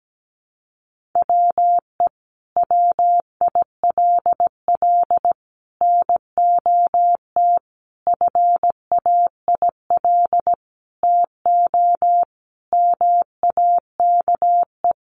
Morse code, heard as a keyed tone, about 17 wpm, 700 Hz. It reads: WE WILL NOT FAIL TO MAKE
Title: Sentences from Top 1100 Words 17wpm